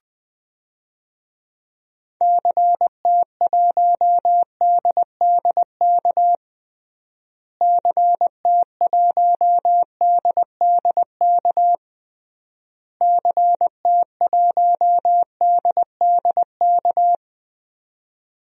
Morse code, heard as a keyed tone, about 20 wpm, 700 Hz. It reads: CT1DDK CT1DDK CT1DDK
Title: DXCC-100 Call Signs - 6 Characters - Repeated 3x - 20wpm